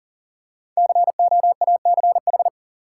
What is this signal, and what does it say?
Morse code, heard as a keyed tone, about 40 wpm, 700 Hz. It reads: COACH
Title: Five Letter Words - 40wpm